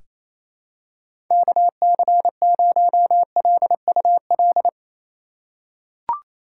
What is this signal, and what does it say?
Morse code, heard as a keyed tone, about 28 wpm, 700 Hz. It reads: KC0LUL
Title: USA Call Signs - 6 Characters - 8x Character Spacing - 28wpm